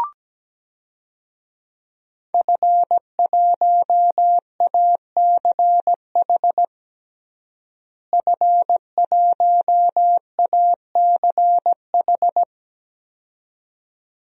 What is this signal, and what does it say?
Morse code, heard as a keyed tone, about 17 wpm, 700 Hz. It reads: F1ACH F1ACH
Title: DXCC-100 Call Signs - 5 Characters - Repeated 2x - 17wpm